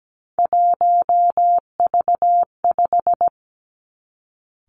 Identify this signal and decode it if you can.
Morse code, heard as a keyed tone, about 17 wpm, 700 Hz. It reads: 1V5
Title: Three Challenging Letter-Number Combinations 17wpm